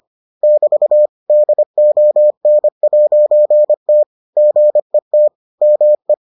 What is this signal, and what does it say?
Morse code, heard as a keyed tone, about 25 wpm, 600 Hz. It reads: <BT> DON'T GET ME